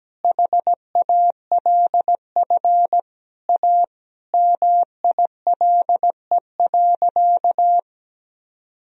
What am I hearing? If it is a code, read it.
Morse code, heard as a keyed tone, about 17 wpm, 700 Hz. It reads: HALF A MILE.
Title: Sentences from Top 1000 Words - Review Part 1 - 17wpm